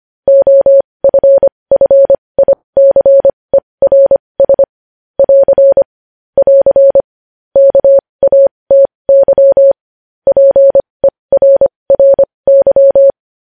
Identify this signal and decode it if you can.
Morse code, heard as a keyed tone, about 25 wpm, spaced as for 20 wpm, 550 Hz. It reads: OFFICERS <AR> <AR> KATY PERRY
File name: podcast / News headlines-1x-25x20